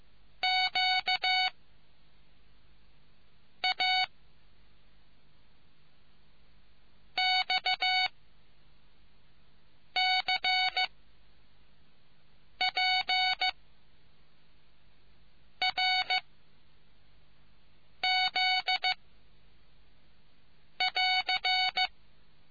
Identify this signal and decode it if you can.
Morse code, heard as a keyed tone, about 15 wpm, 2250 Hz. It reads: Q A X C P R Z <AR>